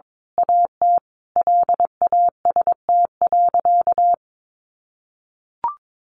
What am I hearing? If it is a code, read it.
Morse code, heard as a keyed tone, about 22 wpm, 700 Hz. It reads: AT LAST.
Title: A Sound Of Thunder 22wpm